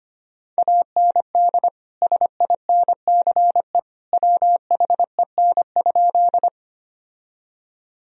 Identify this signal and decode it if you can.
Morse code, heard as a keyed tone, about 25 wpm, 700 Hz. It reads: AND SINCE WHEN?
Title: Sentences from Top 1000 Words - Review Part 1 - 25wpm